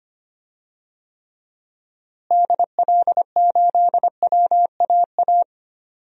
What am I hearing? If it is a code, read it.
Morse code, heard as a keyed tone, about 25 wpm, 700 Hz. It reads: DL8WAA
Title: DXCC-100 Call Signs - 6 Characters - Repeated 2x - 25wpm